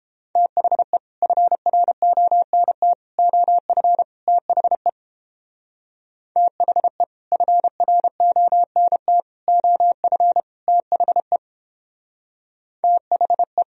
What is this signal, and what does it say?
Morse code, heard as a keyed tone, about 29 wpm, 700 Hz. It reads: THE FRONT OF THE THE FRONT OF THE THE